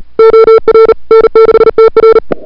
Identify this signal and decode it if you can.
Morse code, heard as a keyed tone, about 34 wpm, 440 Hz. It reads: OR N6TR
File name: CWSprint / February 2007 CW Sprint / 01-2007 CW Sprint Hour1